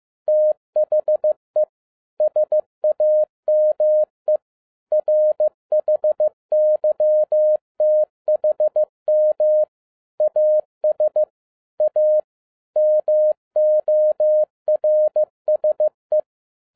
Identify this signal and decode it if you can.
Morse code, heard as a keyed tone, about 15 wpm, 610 Hz. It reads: THE SAME RHYTHM AS A MORSE
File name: MY CW LIFE 1 15wpm